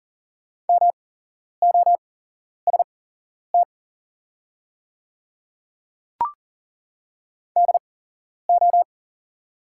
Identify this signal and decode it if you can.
Morse code, heard as a keyed tone, about 39 wpm, spaced as for 11 wpm, 690 Hz. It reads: MOST DO
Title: 8x Character Spacing - Top 200 Words - 40wpm